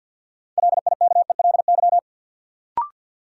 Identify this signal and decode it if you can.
Morse code, heard as a keyed tone, about 53 wpm, 700 Hz. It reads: LIKELY
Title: Sets of 2 Words - New Word Drill - 50wpm